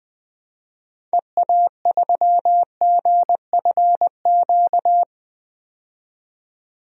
Morse code, heard as a keyed tone, about 20 wpm, 700 Hz. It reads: EA3GFQ
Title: DXCC-100 Call Signs - 6 Characters - 20wpm